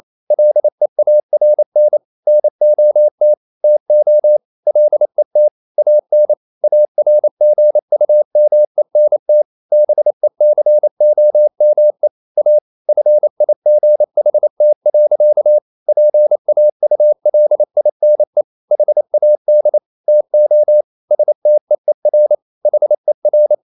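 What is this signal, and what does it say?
Morse code, heard as a keyed tone, about 28 wpm, 600 Hz. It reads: LEARN NOT TO LET AN ARGUMENT BECOME A FIGHT. PAULINE HAD TO STEER HER
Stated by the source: Morse code practice at 28 wpm, text of Ken Follett's novel Never in English, from CW med Gnister.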